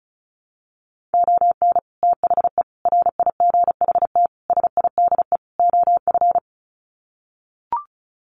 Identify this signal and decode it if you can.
Morse code, heard as a keyed tone, about 34 wpm, 700 Hz. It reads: ON THE RIGHT SIDE OF
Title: Sets of 5 Words Encore 35wpm